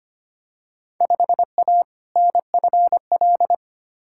Morse code, heard as a keyed tone, about 25 wpm, 700 Hz. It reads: 5A NFL
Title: Field Day Exchanges - 25wpm